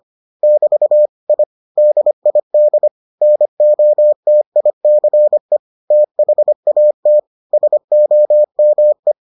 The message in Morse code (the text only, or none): <BT> I DID NOTICE THAT SOME